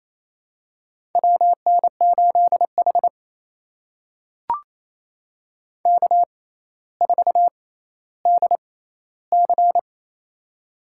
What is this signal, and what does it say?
Morse code, heard as a keyed tone, about 28 wpm, spaced as for 14 wpm, 700 Hz. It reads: WN8H K4DC
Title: USA Call Signs - 4 Characters - 6x Character Spacing - 28wpm